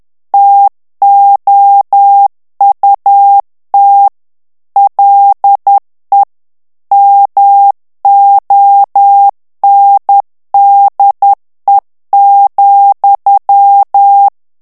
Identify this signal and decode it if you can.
Morse code, heard as a keyed tone, about 10 wpm, 800 Hz. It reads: TOUT LE MONDE,
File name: FRANCAIS TEXTE CLAIR 2 VITESSE 10 MOTS MINUTES